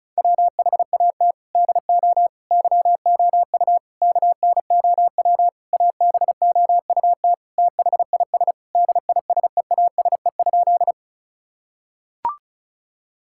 Morse code, heard as a keyed tone, about 36 wpm, 700 Hz. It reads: WHAT DO YOU KNOW ABOUT THIS DISEASE?
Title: Sentences from Top 1000 Words - Review Part 6 - 35wpm